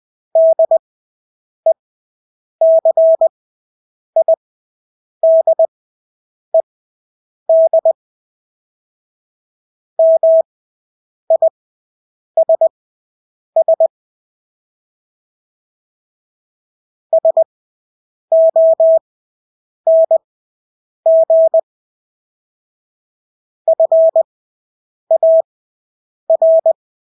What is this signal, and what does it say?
Morse code, heard as a keyed tone, about 20 wpm, spaced as for 8 wpm, 650 Hz. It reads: DECIDED MISS SONG FAR
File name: Engelske ord 20/8 wpm